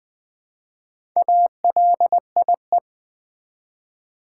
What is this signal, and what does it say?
Morse code, heard as a keyed tone, about 20 wpm, 700 Hz. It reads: ALIE
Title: Build A Word - 1 Letter At A Time - 5-Letter Words - 20wpm